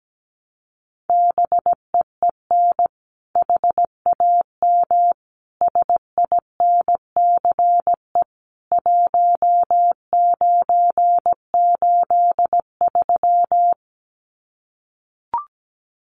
Morse code, heard as a keyed tone, about 17 wpm, 700 Hz. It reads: BEEN HAM SINCE 1983
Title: Common QSO Phrases 17wpm